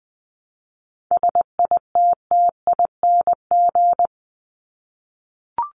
Morse code, heard as a keyed tone, about 20 wpm, 700 Hz. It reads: SITTING
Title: ING Suffix - Top 100 Words with Base Words 20wpm